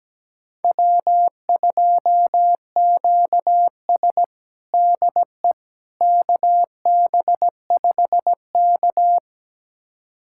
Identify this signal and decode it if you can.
Morse code, heard as a keyed tone, about 17 wpm, 700 Hz. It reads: W2QS DE KB5K
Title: USA Callsign DE Callsign - 4 Characters - 17wpm